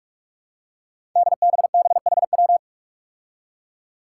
Morse code, heard as a keyed tone, about 44 wpm, 700 Hz. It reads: DBBHW
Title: Five Challenging Letter-Number Combinations 45wpm